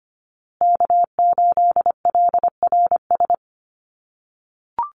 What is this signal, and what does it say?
Morse code, heard as a keyed tone, about 25 wpm, 700 Hz. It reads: K8LRS
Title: USA Call Signs Repeated 2x 25wpm